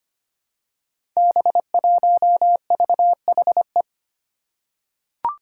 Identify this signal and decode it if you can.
Morse code, heard as a keyed tone, about 25 wpm, 700 Hz. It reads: B1VHE
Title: Five Challenging Letter-Number Combinations 25wpm